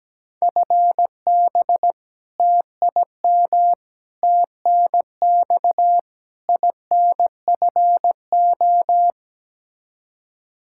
Morse code, heard as a keyed tone, about 17 wpm, 700 Hz. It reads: FB TIM TNX INFO
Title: K1USN SST QSO Practice 17wpm